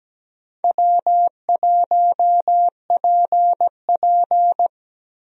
Morse code, heard as a keyed tone, about 17 wpm, 700 Hz. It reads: W1PP
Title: USA Call Signs - 4 Characters - Repeated 2x - 17wpm